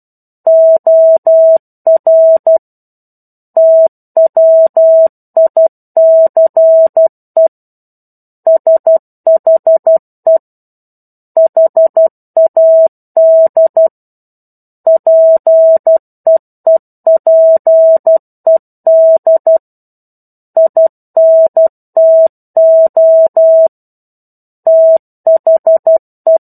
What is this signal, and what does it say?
Morse code, heard as a keyed tone, about 12 wpm, 650 Hz. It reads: OR TWICE SHE HAD PEEPED INTO THE
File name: 01-Chapter-01 12 WPM